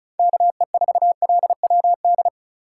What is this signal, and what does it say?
Morse code, heard as a keyed tone, about 36 wpm, 700 Hz. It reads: KE4LWD
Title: USA Call Signs - 6 Characters - 2x Character Spacing - 35wpm